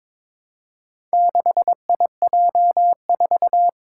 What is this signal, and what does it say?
Morse code, heard as a keyed tone, about 22 wpm, 700 Hz. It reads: 6IJ4
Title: Four Challenging Letter-Number Combinations 22wpm